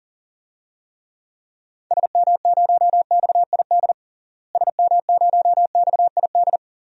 Morse code, heard as a keyed tone, about 40 wpm, 700 Hz. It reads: SM0XID SM0XID
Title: DXCC-100 Call Signs - 6 Characters - Repeated 2x - 40wpm